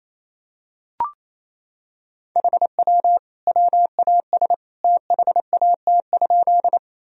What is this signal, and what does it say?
Morse code, heard as a keyed tone, about 28 wpm, 700 Hz. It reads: HW WAS THAT?
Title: Sentences from Top 100 Words and Abbreviations 28wpm